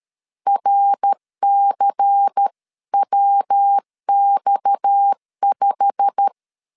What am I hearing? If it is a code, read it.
Morse code, heard as a keyed tone, about 13 wpm, 800 Hz. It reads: RCWX5
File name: CW / DET / 1 9 1500C S C 12wpm 3811